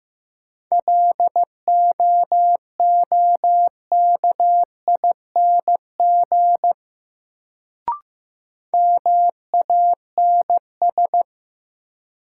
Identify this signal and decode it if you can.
Morse code, heard as a keyed tone, about 15 wpm, 700 Hz. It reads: LOOKING MANS
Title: Sentences from Top 1000 Words - Review Part 6 - New Word Drill - 15wpm